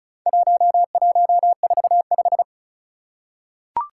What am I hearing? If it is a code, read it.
Morse code, heard as a keyed tone, about 36 wpm, 700 Hz. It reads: 1145
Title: Four Challenging Letter-Number Combinations 35wpm